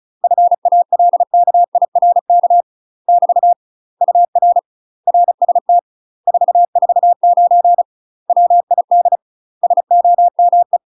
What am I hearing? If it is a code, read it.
Morse code, heard as a keyed tone, about 35 wpm, 700 Hz. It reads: FALKIRK <BT> UR RST 449 WID SOME
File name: UK MASTER QSOs TRAINING / 35WPM / UK MASTER QSOs 0042